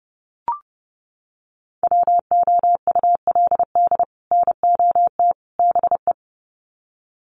Morse code, heard as a keyed tone, about 30 wpm, 700 Hz. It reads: WOULD NOT BE
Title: Sets of 3 Words 30wpm